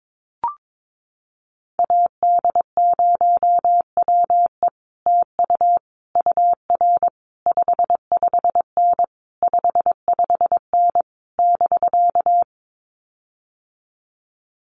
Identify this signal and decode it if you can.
Morse code, heard as a keyed tone, about 22 wpm, 700 Hz. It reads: AD0WE TU UR 55N 55N <BK>
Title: Common QSO Phrases Encore 22wpm